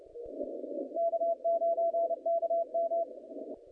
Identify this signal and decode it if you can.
Morse code, heard as a keyed tone, about 31 wpm, 670 Hz. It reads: K9KM
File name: contests / naqp-cw-k5tr-op-k5pi- -0300